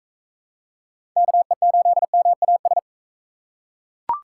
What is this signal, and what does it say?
Morse code, heard as a keyed tone, about 45 wpm, 700 Hz. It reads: KE8MAS